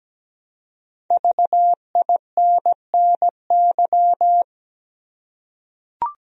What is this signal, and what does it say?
Morse code, heard as a keyed tone, about 17 wpm, 700 Hz. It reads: VINNY
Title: USA Common Names - Max 5 Letters - 17wpm